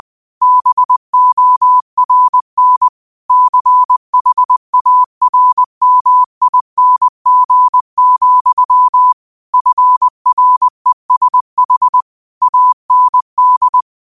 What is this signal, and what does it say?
Morse code, heard as a keyed tone, about 20 wpm, 1000 Hz. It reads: BORN CHARMING, FRESH AND
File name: qotdfiles / 20wpm